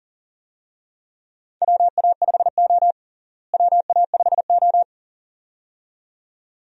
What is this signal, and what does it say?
Morse code, heard as a keyed tone, about 40 wpm, 700 Hz. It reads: WA5O WA5O
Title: USA Call Signs - 4 Characters - Repeated 2x - 40wpm